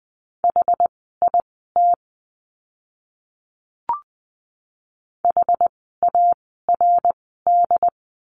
Morse code, heard as a keyed tone, about 20 wpm, spaced as for 15 wpm, 700 Hz. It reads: HIT HARD
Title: 068 Lesson 12f - Letter d - Words 20wpm